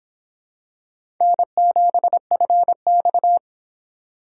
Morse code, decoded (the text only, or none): N7FX